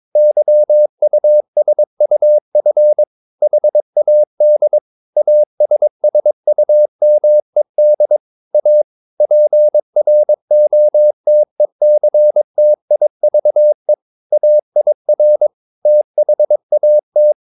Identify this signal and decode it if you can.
Morse code, heard as a keyed tone, about 22 wpm, 600 Hz. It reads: YUSUF HAD ASSUMED A PROTECTIVE AIR THAT
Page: Romanen Never af Ken Follet på engelsk.